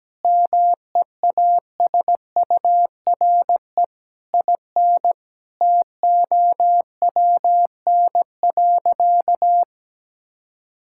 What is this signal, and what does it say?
Morse code, heard as a keyed tone, about 17 wpm, 700 Hz. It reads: MEASURE IN TOWN.
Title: Sentences from Top 1000 Words - Review Part 5 - 17wpm